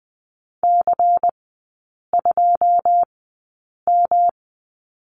Morse code, heard as a keyed tone, about 20 wpm, spaced as for 8 wpm, 700 Hz. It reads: C2M
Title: 172 Lesson 28e - Number 9 - Three Characters 20wpm